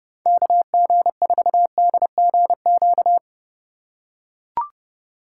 Morse code, heard as a keyed tone, about 28 wpm, 700 Hz. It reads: KG4DGQ